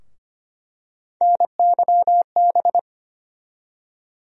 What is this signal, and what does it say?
Morse code, heard as a keyed tone, about 25 wpm, 700 Hz. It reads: NYB
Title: USA Call Sign Postfixes - 3 Letters 25wpm